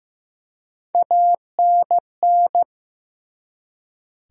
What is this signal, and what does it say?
Morse code, heard as a keyed tone, about 15 wpm, 700 Hz. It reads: ANN
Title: USA Common Names - Max 3 Letters - 15wpm